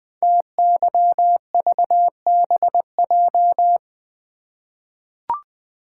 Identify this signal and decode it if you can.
Morse code, heard as a keyed tone, about 20 wpm, 700 Hz. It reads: TYVBJ
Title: Code Groups - 5 Letters - 20wpm